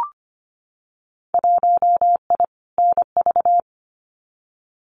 Morse code, heard as a keyed tone, about 25 wpm, 700 Hz. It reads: 1I NV
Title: Winter Field Day Exchanges Encore 25wpm